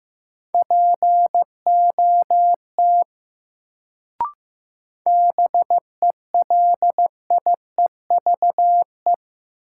Morse code, heard as a keyed tone, about 15 wpm, 700 Hz. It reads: POT BELIEVE
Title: Sentences from Top 1000 Words - New Word Drill - 15wpm